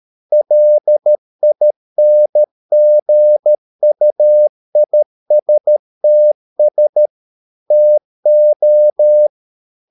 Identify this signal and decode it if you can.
Morse code, heard as a keyed tone, about 13 wpm, 600 Hz. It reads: LINGUISTS TO